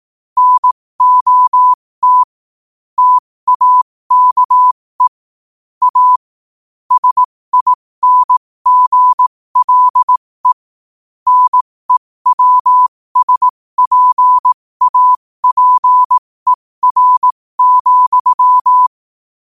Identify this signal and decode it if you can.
Morse code, heard as a keyed tone, about 18 wpm, spaced as for 15 wpm, 1000 Hz. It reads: NOT TAKE A SINGLE NEWSPAPER,